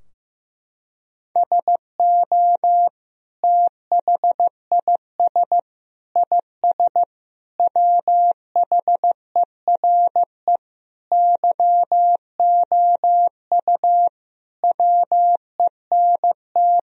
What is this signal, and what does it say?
Morse code, heard as a keyed tone, about 15 wpm, 700 Hz. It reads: SO THIS IS WHERE YOU WENT
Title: Sentences from Top 1000 Words - Review Part 2 - 15wpm